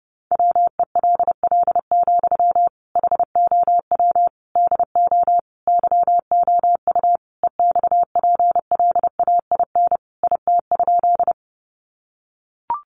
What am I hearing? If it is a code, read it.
Morse code, heard as a keyed tone, about 29 wpm, 700 Hz. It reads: WELL, HOW DO YOU EXPLAIN IT?
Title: Sentences from Top 1100 Words 30wpm